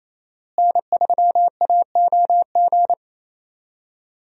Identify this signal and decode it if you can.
Morse code, heard as a keyed tone, about 28 wpm, 700 Hz. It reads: N3AOG
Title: USA Call Signs - 5 Characters - Repeated 2x - 28wpm